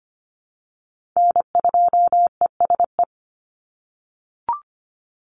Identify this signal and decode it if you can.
Morse code, heard as a keyed tone, about 25 wpm, 700 Hz. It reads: N2ESE
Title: NA SOTA Chaser Callsigns 25wpm